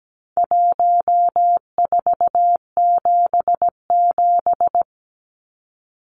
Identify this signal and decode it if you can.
Morse code, heard as a keyed tone, about 17 wpm, 700 Hz. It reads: 1477
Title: Numbers - Contest Style - 4-Digit 17wpm